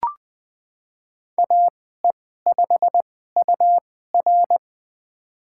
Morse code, heard as a keyed tone, about 20 wpm, spaced as for 14 wpm, 700 Hz. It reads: AE5UR
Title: USA Call Signs - 5 Characters - 2x Character Spacing - 20wpm